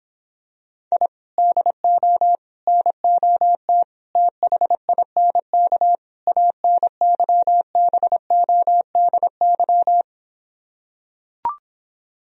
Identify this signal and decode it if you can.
Morse code, heard as a keyed tone, about 26 wpm, 700 Hz. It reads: I DO NOT THINK ANYBODY